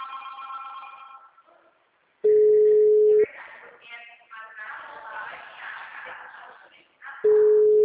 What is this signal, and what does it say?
Ringback tone (Europe, 425 Hz), with speech.
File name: out- -2045- - - .4758